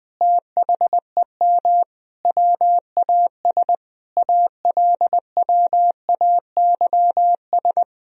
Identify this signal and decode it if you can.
Morse code, heard as a keyed tone, about 20 wpm, 700 Hz. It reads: THEM WAS ALWAYS
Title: Sentences from Top 1000 Words - Review Part 1 - 20wpm